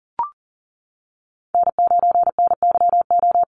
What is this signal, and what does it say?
Morse code, heard as a keyed tone, about 40 wpm, 700 Hz. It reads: N9NYO
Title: USA Call Signs Repeated 2x 40wpm